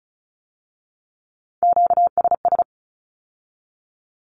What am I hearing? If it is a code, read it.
Morse code, heard as a keyed tone, about 36 wpm, 700 Hz. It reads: QSS
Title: Q-Codes with Spoken Meaning - 35wpm